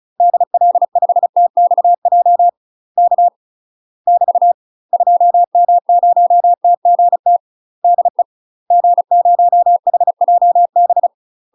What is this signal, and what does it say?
Morse code, heard as a keyed tone, about 36 wpm, 700 Hz. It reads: DL5TXJ K <BT> 2M0TGT DE G0HJB
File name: UK MASTER QSOs TRAINING / 35WPM / UK MASTER QSOs 0011